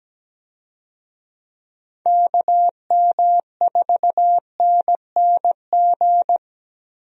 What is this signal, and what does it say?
Morse code, heard as a keyed tone, about 17 wpm, 700 Hz. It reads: KM4NNG
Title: USA Call Signs - 6 Characters - Repeated 2x - 17wpm